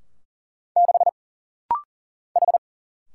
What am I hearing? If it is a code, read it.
Morse code, heard as a keyed tone, about 40 wpm, 700 Hz. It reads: 6 H
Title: Single Letter-Number Alt - Rapid-Fire 40wpm